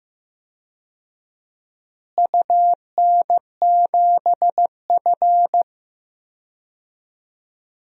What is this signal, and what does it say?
Morse code, heard as a keyed tone, about 15 wpm, 700 Hz. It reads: UN7F
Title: DXCC-100 Call Signs - 4 Characters - Repeated 2x - 15wpm